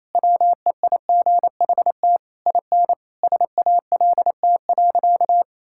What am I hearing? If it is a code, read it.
Morse code, heard as a keyed tone, about 28 wpm, 700 Hz. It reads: WEIGHT IN SALT.
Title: Sentences from Top 1000 Words - Review Part 5 - 28wpm